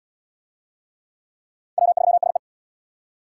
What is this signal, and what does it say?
Morse code, heard as a keyed tone, about 68 wpm, 700 Hz. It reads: W3SE